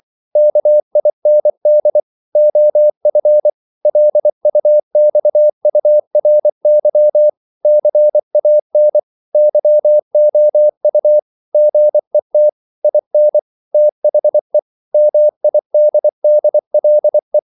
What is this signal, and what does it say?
Morse code, heard as a keyed tone, about 24 wpm, 600 Hz. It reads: KIND OF LUXURY CAN YOU GET IN THE MIDDLE